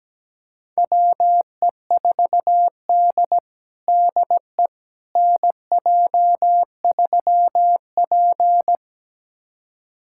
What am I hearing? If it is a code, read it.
Morse code, heard as a keyed tone, about 17 wpm, 700 Hz. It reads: WE4D DE NJ3P
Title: USA Callsign DE Callsign - 4 Characters - 17wpm